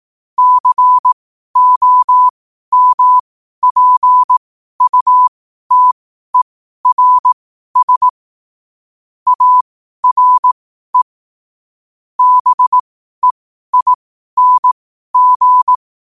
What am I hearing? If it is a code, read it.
Morse code, heard as a keyed tone, about 18 wpm, spaced as for 12 wpm, 1000 Hz. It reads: COMPUTERS ARE BEING
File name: qotdfiles / 12wpm